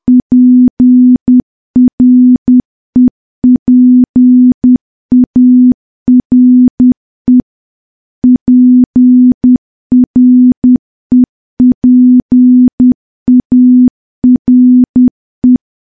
Morse code, heard as a keyed tone, about 10 wpm, 250 Hz. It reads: PREPARE PREPARE
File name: morse / morse-training-10WPM